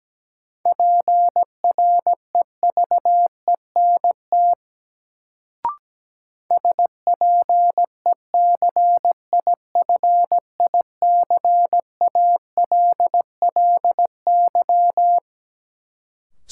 Morse code, heard as a keyed tone, about 17 wpm, 700 Hz. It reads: PREVENT SPECIFICALLY
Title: Sentences from Top 1100 Words - New Word Drill Alt - 17wpm